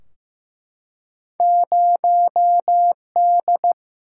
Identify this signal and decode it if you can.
Morse code, heard as a keyed tone, about 15 wpm, 700 Hz. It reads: 0 D
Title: Single-Letter Single-Number 15wpm